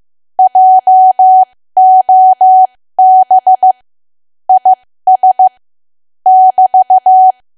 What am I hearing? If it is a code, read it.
Morse code, heard as a keyed tone, about 15 wpm, spaced as for 13 wpm, 750 Hz. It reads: JOB IS <BT>
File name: cwa / W1AW-Level2 / 13WPM